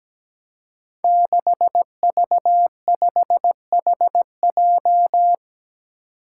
Morse code, heard as a keyed tone, about 17 wpm, 700 Hz. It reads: 6V5HJ
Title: Five Challenging Letter-Number Combinations 17wpm